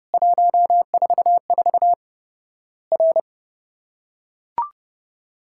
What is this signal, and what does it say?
Morse code, heard as a keyed tone, about 30 wpm, 700 Hz. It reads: 144 R
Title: Worked All Europe DX Contest - QTC Exchange - 30wpm